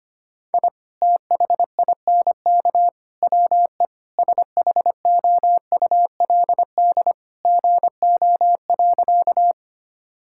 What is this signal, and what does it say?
Morse code, heard as a keyed tone, about 25 wpm, 700 Hz. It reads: I THINK WE SHOULD GO.
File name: practice / 25wpm / Sentences with 3-Word Phrases - 25wpm